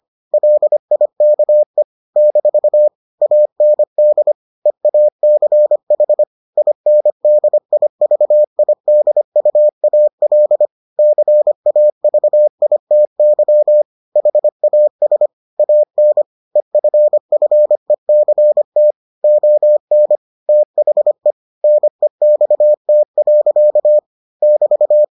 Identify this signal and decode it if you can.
Morse code, heard as a keyed tone, about 25 wpm, 600 Hz. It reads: LIKE - AND EACH INDIVIDUAL CAVITY HAS AN EFFECT ON THE NEXT. <BT>